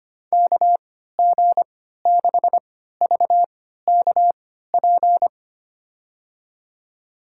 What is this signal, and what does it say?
Morse code, heard as a keyed tone, about 25 wpm, spaced as for 14 wpm, 700 Hz. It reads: KG6VKP
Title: USA Call Signs - 3x Character Spacing - 25wpm